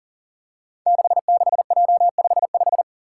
Morse code, heard as a keyed tone, about 41 wpm, 700 Hz. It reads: 66J55
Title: Five Challenging Letter-Number Combinations 40wpm